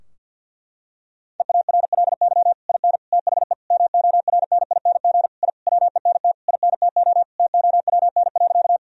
Morse code, heard as a keyed tone, about 50 wpm, 700 Hz. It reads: EARLY IN THE MORNING I WENT INTO TOWN.
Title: Sentences from Top 1000 Words - Review Part 2 - 50wpm